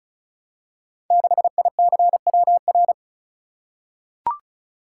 Morse code, heard as a keyed tone, about 36 wpm, 700 Hz. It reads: 6ICWR
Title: Code Groups - 5 Letter:Numbers - 35wpm